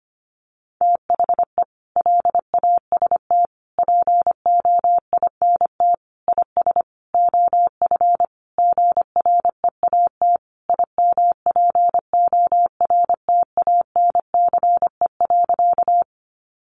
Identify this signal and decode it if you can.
Morse code, heard as a keyed tone, about 25 wpm, 700 Hz. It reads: THE LAST POINT IS OF GREAT IMPORTANCE.
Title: Sentences from Top 1200 Words 25wpm